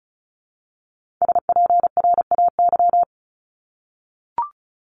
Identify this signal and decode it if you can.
Morse code, heard as a keyed tone, about 36 wpm, 700 Hz. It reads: SPRAY
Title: Multi-letter Phonemes 35wpm